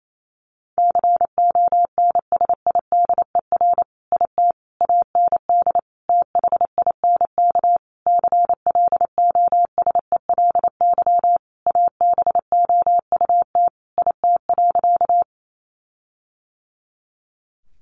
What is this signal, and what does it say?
Morse code, heard as a keyed tone, about 28 wpm, 700 Hz. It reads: CONSIDER IT AND THINK CLOSELY ABOUT IT.
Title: Sentences from Top 1400 Words 28wpm